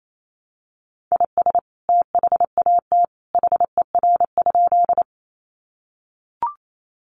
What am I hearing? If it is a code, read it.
Morse code, heard as a keyed tone, about 28 wpm, 700 Hz. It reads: IS THAT HER?
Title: Sentences from Top 100 Words 28wpm